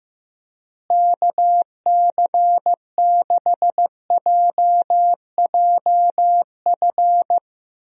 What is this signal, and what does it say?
Morse code, heard as a keyed tone, about 16 wpm, 700 Hz. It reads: KC6JJF
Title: USA Call Signs - 6 Characters - Repeated 2x - 15wpm